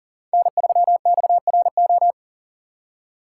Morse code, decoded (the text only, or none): N3XRO